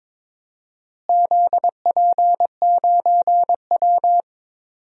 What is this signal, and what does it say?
Morse code, heard as a keyed tone, about 22 wpm, 700 Hz. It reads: ZP9W
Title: DXCC-100 Call Signs - 4 Characters - Repeated 3x - 22wpm